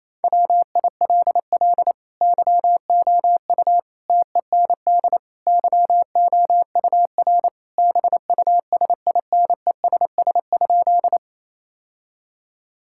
Morse code, heard as a keyed tone, about 28 wpm, 700 Hz. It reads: WILL YOU TEND YOUR BUSINESS?
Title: Sentences from Top 1000 Words - Review Part 5 - 28wpm